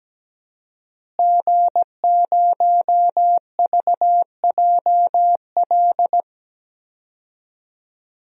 Morse code, heard as a keyed tone, about 17 wpm, 700 Hz. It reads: G0VJL
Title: DXCC-100 Call Signs - 5 Characters - Repeated 3x - 17wpm